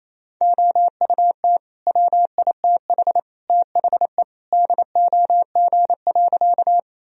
Morse code, heard as a keyed tone, about 28 wpm, 700 Hz. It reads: OUT WITH THE DOG.
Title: Sentences from Top 1000 Words - Review Part 2 - 28wpm